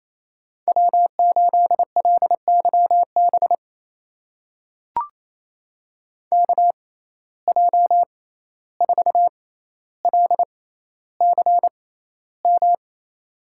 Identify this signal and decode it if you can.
Morse code, heard as a keyed tone, about 28 wpm, spaced as for 14 wpm, 700 Hz. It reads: W8LYB KJ4LCM
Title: USA Call Signs - 6x Character Spacing - 28wpm